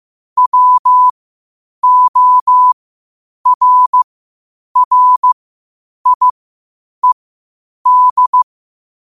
Morse code, heard as a keyed tone, about 15 wpm, spaced as for 8 wpm, 1000 Hz. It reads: WORRIED
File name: qotdfiles / 8wpm